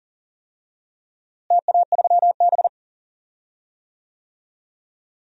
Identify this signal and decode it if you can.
Morse code, heard as a keyed tone, about 40 wpm, 700 Hz. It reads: TA3B
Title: DXCC-100 Call Signs - 4 Characters - Repeated 3x - 40wpm